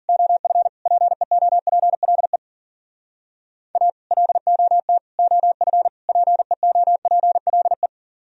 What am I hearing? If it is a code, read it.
Morse code, heard as a keyed tone, about 46 wpm, 700 Hz. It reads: OF PEOPLE A LOT OF PEOPLE
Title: Speed-Racing - Sets of 4 Words 40wpm